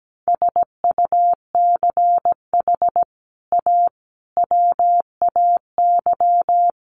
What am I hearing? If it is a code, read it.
Morse code, heard as a keyed tone, about 17 wpm, 700 Hz. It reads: SUCH A WAY
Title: Sets of 5 Words Encore2 17wpm